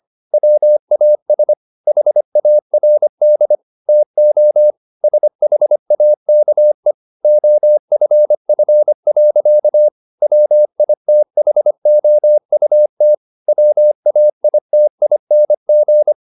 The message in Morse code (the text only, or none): WAS HARD TO SHAKE OFF. WITHOUT WAITING